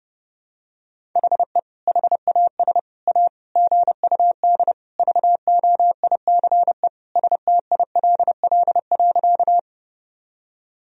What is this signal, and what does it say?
Morse code, heard as a keyed tone, about 30 wpm, 700 Hz. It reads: HE HAS A GUD VOICE STILL.
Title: Sentences from Top 500 Words and Abbreviations 30wpm